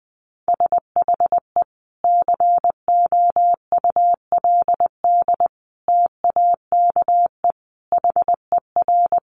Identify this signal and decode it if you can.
Morse code, heard as a keyed tone, about 20 wpm, 700 Hz. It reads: SHE COULD TAKE HER
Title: Sentences from Top 900 Words 20wpm